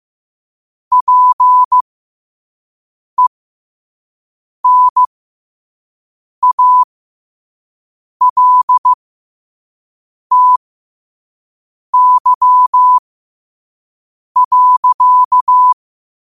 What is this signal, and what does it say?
Morse code, heard as a keyed tone, about 15 wpm, spaced as for 5 wpm, 1000 Hz. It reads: PENALTY.
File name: qotdfiles / 5wpm